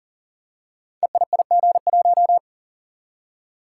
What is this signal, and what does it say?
Morse code, heard as a keyed tone, about 41 wpm, 700 Hz. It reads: EIIG1
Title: Code Groups - 5 Letter:Numbers - 40wpm